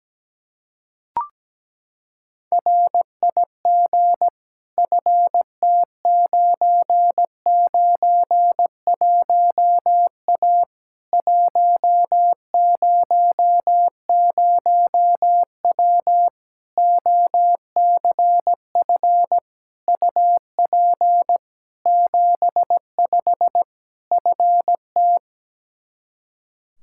Morse code, heard as a keyed tone, about 17 wpm, 700 Hz. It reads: RIG FT991A 100W OCF UP 75 FT
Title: K1USN SST QSO Practice 17wpm